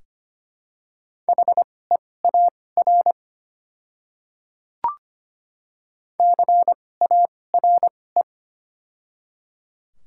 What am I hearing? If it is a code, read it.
Morse code, heard as a keyed tone, about 25 wpm, spaced as for 18 wpm, 700 Hz. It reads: HEAR CARE
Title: 2x Character Spacing - Top 500 Words - 25wpm